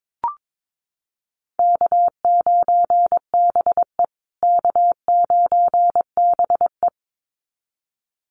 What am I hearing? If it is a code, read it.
Morse code, heard as a keyed tone, about 22 wpm, 700 Hz. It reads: K9BE K9BE
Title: USA Call Signs Repeated 2x 22wpm